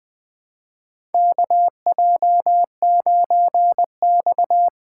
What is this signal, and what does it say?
Morse code, heard as a keyed tone, about 21 wpm, 700 Hz. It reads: KJ9X
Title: 2023 SOTA Chaser Call Signs - 20wpm